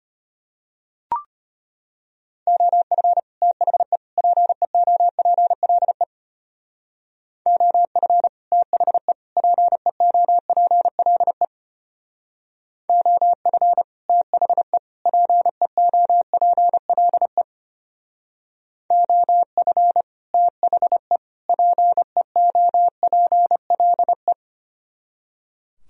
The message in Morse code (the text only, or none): OF THE PEOPLE OF THE PEOPLE OF THE PEOPLE OF THE PEOPLE